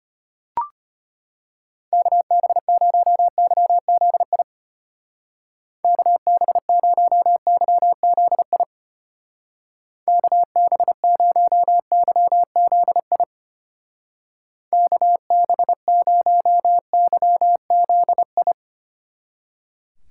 Morse code, heard as a keyed tone, about 32 wpm, 700 Hz. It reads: KB0YZI KB0YZI KB0YZI KB0YZI